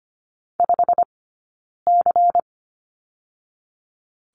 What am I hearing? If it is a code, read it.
Morse code, heard as a keyed tone, about 25 wpm, 700 Hz. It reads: 5 C
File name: course / 25:10wpm / 092 Lesson 16d - Letter c - Two Characters 25wpm